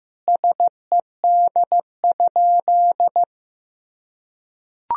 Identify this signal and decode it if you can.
Morse code, heard as a keyed tone, about 15 wpm, 700 Hz. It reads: SED?
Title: Sentences from Top 100 Words and Abbreviations 15wpm